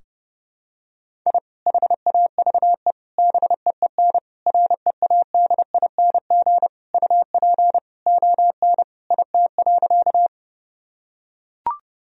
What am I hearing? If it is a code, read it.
Morse code, heard as a keyed tone, about 32 wpm, 700 Hz. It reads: I HAVE BEEN READING UP ON IT.
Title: Sentences from Top 1000 Words - Review Part 3 - 30wpm